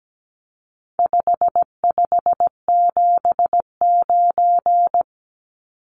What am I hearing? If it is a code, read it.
Morse code, heard as a keyed tone, about 17 wpm, 700 Hz. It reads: 5579
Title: Numbers - 4-Digit 17wpm